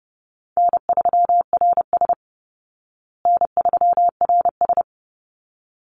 Morse code, heard as a keyed tone, about 30 wpm, 700 Hz. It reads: N3RS N3RS
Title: Call Signs Repeated 3x - 30wpm